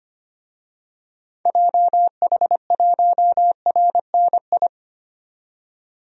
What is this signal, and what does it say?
Morse code, heard as a keyed tone, about 25 wpm, 700 Hz. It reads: JH1RNI
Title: DXCC-100 Call Signs - 6 Characters - 25wpm